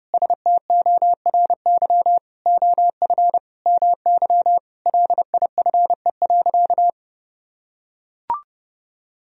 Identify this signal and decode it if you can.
Morse code, heard as a keyed tone, about 28 wpm, 700 Hz. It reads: STORY OF MY LIFE.
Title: Sentences from Top 1000 Words - Review Part 1 - 30wpm